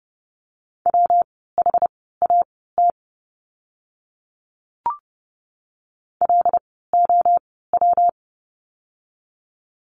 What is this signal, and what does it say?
Morse code, heard as a keyed tone, about 30 wpm, spaced as for 17 wpm, 700 Hz. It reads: WHAT LOW
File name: course / 30:10wpm / 115 Lesson 19f - Letter w - Words 30wpm